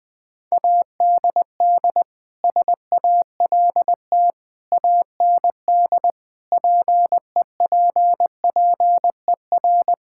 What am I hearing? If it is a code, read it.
Morse code, heard as a keyed tone, about 20 wpm, 700 Hz. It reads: ADD SALT AND PEPPER
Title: Sentences with 5-Word Phrases - 20wpm